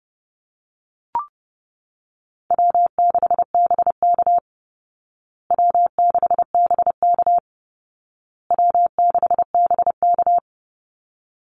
Morse code, heard as a keyed tone, about 30 wpm, 700 Hz. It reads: W6BK W6BK W6BK
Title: Easy Call Signs Repeated 3x - 30wpm